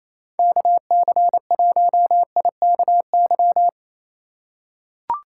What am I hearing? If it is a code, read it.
Morse code, heard as a keyed tone, about 28 wpm, 700 Hz. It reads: KC1IKY
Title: USA Call Signs - 6 Characters - 3x Character Spacing - 28wpm